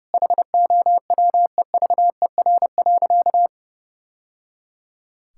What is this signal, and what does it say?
Morse code, heard as a keyed tone, about 30 wpm, 700 Hz. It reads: HOWEVER.
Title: Sentences from Top 1000 Words - Review Part 6 - 30wpm